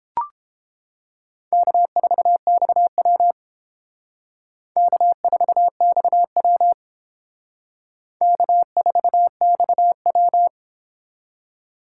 Morse code, heard as a keyed tone, about 28 wpm, 700 Hz. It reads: K4XW K4XW K4XW